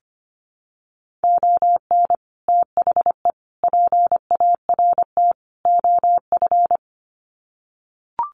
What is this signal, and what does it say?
Morse code, heard as a keyed tone, about 25 wpm, 700 Hz. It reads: ON THE PART OF
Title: Sets of 4 Words 25wpm